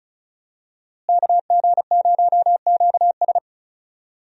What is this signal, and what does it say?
Morse code, heard as a keyed tone, about 36 wpm, 700 Hz. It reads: KG0QS
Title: USA Call Signs - 5 Characters - Repeated 2x - 35wpm